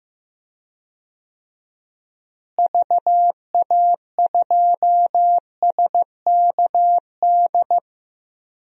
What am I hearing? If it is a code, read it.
Morse code, heard as a keyed tone, about 15 wpm, 700 Hz. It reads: VA2SKD
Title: DXCC-100 Call Signs - 6 Characters - Repeated 2x - 15wpm